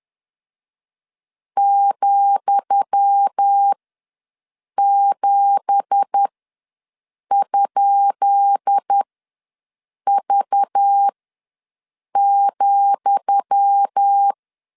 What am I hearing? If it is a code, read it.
Morse code, heard as a keyed tone, about 10 wpm, spaced as for 6 wpm, 800 Hz. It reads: ,7?V,
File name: LEZIONI / 10 SIGNS / 10D COMA ws 10wpm 4314